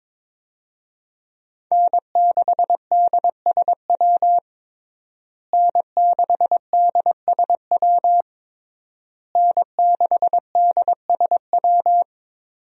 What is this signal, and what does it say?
Morse code, heard as a keyed tone, about 22 wpm, 700 Hz. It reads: N6DSW N6DSW N6DSW
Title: USA Call Signs - 5 Characters - Repeated 3x - 22wpm